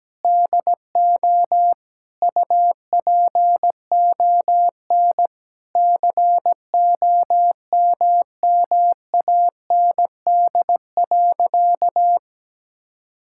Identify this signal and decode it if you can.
Morse code, heard as a keyed tone, about 17 wpm, 700 Hz. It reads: DO UPON COMMAND.
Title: Sentences from Top 1600 Words 17wpm